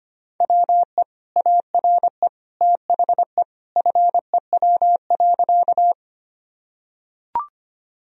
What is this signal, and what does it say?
Morse code, heard as a keyed tone, about 25 wpm, 700 Hz. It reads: WE ARE THE FEW.
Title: Sentences from Top 1000 Words - Review Part 1 - 25wpm